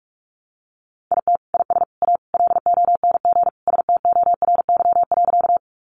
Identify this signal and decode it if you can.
Morse code, heard as a keyed tone, about 44 wpm, 700 Hz. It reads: IT IS A LONG STORY.
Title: Sentences from Top 300 Words 45wpm